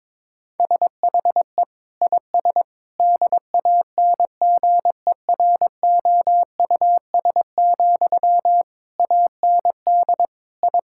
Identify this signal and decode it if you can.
Morse code, heard as a keyed tone, about 22 wpm, 700 Hz. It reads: SHE IS DANGEROUS, AND I
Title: Sentences from Top 1600 Words 22wpm